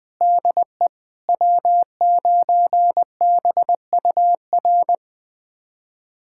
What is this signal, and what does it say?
Morse code, heard as a keyed tone, about 21 wpm, 700 Hz. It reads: DE W9BUR
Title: USA Callsign DE Callsign 20wpm